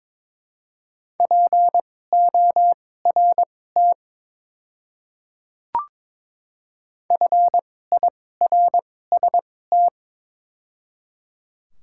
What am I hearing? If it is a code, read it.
Morse code, heard as a keyed tone, about 22 wpm, spaced as for 16 wpm, 700 Hz. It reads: PORT FIRST
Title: 2x Character Spacing - Top 200 Words - 22wpm